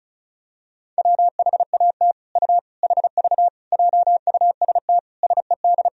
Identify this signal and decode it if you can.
Morse code, heard as a keyed tone, about 34 wpm, 700 Hz. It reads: WHAT U HV JUST SED
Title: Sentences from Top 200 Words and Abbreviations 35wpm